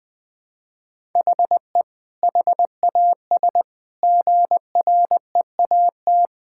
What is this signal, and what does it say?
Morse code, heard as a keyed tone, about 20 wpm, 700 Hz. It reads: HE HAS GREAT
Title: Sentences from Top 1000 Words - Review Part 6 - 20wpm